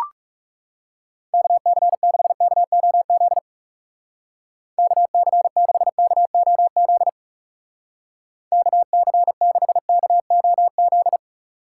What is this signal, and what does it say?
Morse code, heard as a keyed tone, about 38 wpm, 700 Hz. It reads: KC6KOZ KC6KOZ KC6KOZ